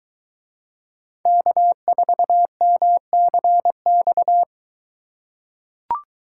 Morse code, heard as a keyed tone, about 23 wpm, 700 Hz. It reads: K4MCX